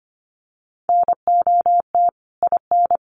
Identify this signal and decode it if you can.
Morse code, heard as a keyed tone, about 25 wpm, 700 Hz. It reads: NOT IN
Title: Sentences from Top 1400 Words 25wpm